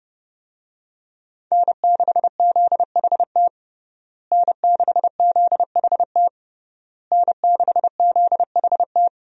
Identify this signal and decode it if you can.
Morse code, heard as a keyed tone, about 30 wpm, 700 Hz. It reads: N6ZHT N6ZHT N6ZHT
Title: USA Call Signs - 5 Characters - Repeated 3x - 30wpm